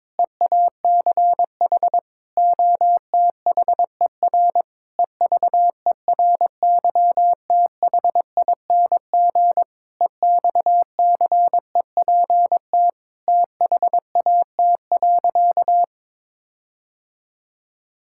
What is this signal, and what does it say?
Morse code, heard as a keyed tone, about 22 wpm, 700 Hz. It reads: EACH OTHER EVERYTHING EXCEPT THAT.
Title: Sentences from Top 1600 Words 22wpm